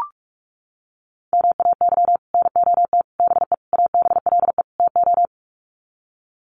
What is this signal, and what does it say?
Morse code, heard as a keyed tone, about 44 wpm, 700 Hz. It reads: MAY NOT BE ABLE TO
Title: Sets of 5 Words 45wpm